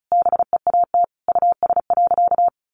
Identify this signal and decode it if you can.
Morse code, heard as a keyed tone, about 34 wpm, 700 Hz. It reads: BEAT US.
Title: Sentences from Top 1300 Words 35wpm